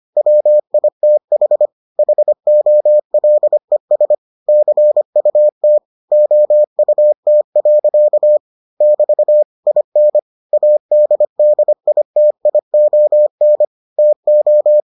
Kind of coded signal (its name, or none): Morse code